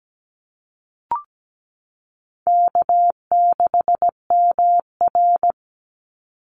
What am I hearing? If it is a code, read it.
Morse code, heard as a keyed tone, about 17 wpm, 700 Hz. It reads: K6MR
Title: Call Signs - 17wpm